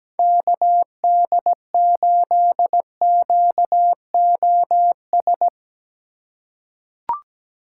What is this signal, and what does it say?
Morse code, heard as a keyed tone, about 17 wpm, 700 Hz. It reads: KD8QOS
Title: USA Call Signs - 6 Characters - 2x Character Spacing - 17wpm